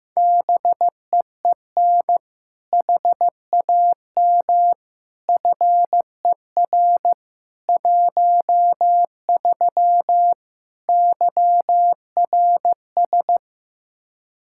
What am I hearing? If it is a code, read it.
Morse code, heard as a keyed tone, about 15 wpm, 700 Hz. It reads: BEEN HAM FER 13 YRS
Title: Practice QSO - Exchange 3 - More Info - 15wpm